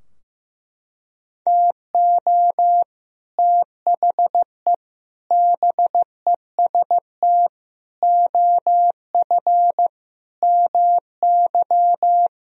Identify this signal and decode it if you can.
Morse code, heard as a keyed tone, about 15 wpm, 700 Hz. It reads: TO THE BEST OF MY
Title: Sentences from Top 1000 Words - Review Part 3 - 15wpm